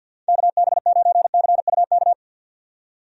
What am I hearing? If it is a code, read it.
Morse code, heard as a keyed tone, about 52 wpm, 700 Hz. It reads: KB9CUK
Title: USA Call Signs - 6 Characters - 50wpm